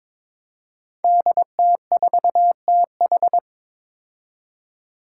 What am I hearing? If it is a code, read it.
Morse code, heard as a keyed tone, about 22 wpm, 700 Hz. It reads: DT4TH
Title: Five Challenging Letter-Number Combinations 22wpm